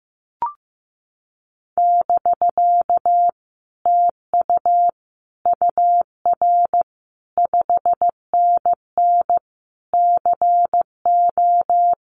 Morse code, heard as a keyed tone, about 15 wpm, 700 Hz. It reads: <BK> TU UR 5NN CO
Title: Common QSO Phrases Encore 15wpm